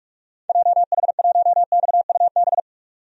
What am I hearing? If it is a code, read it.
Morse code, heard as a keyed tone, about 46 wpm, 700 Hz. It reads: JH1XUB
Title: DXCC-100 Call Signs - 6 Characters - 45wpm